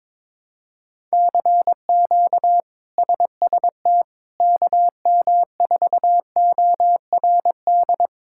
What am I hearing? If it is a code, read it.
Morse code, heard as a keyed tone, about 22 wpm, 700 Hz. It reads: CQ SST KM4ORD
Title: K1USN SST QSO Practice - 22wpm